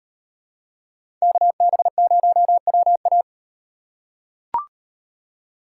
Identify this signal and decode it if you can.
Morse code, heard as a keyed tone, about 36 wpm, 700 Hz. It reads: KB0WA